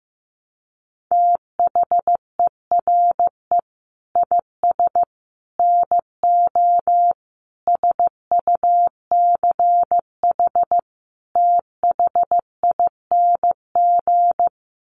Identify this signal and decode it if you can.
Morse code, heard as a keyed tone, about 15 wpm, 700 Hz. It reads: THERE IS NO SUCH THING
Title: Sets of 5 Words Encore 15wpm